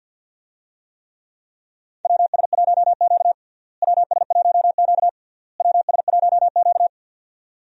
Morse code, heard as a keyed tone, about 50 wpm, 700 Hz. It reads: WS1Q WS1Q WS1Q
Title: USA Call Signs - 4 Characters - Repeated 3x - 50wpm